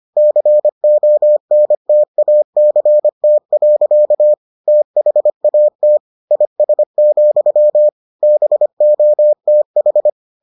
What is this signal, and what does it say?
Morse code, heard as a keyed tone, about 25 wpm, 600 Hz. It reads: CONTACT. THAT IS, BOTH